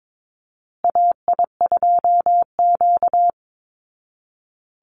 Morse code, heard as a keyed tone, about 22 wpm, 690 Hz. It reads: AI2Q
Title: Call Signs Repeated 2x - 22wpm